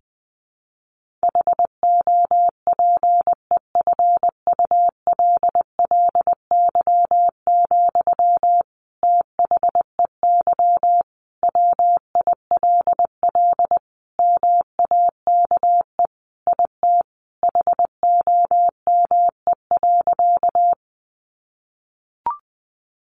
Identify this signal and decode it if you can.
Morse code, heard as a keyed tone, about 20 wpm, 700 Hz. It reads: HOPEFULLY, THEY WILL MAKE IT HOME.
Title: Sentences from Top 1500 Words 20wpm